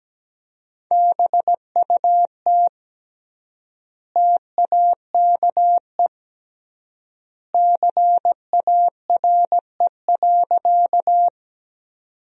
Morse code, heard as a keyed tone, about 17 wpm, 700 Hz. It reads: BUT TAKE CARE.
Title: 3x Word-Spacing - Sentences from Top 300 Words - 17wpm